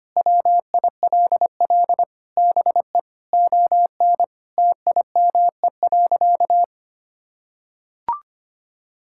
Morse code, heard as a keyed tone, about 25 wpm, 700 Hz. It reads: WILL BE ON TIME.
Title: Sentences from Top 1000 Words - Review Part 6 - 25wpm